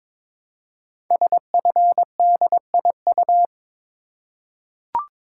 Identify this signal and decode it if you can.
Morse code, heard as a keyed tone, about 23 wpm, 700 Hz. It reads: SFDIU
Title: Code Groups - 5 Letters - 22wpm